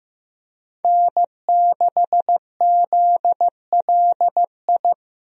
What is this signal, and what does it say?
Morse code, heard as a keyed tone, about 15 wpm, 700 Hz. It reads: N6ZLI
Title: USA Call Signs - 5 Characters - Repeated 2x - 15wpm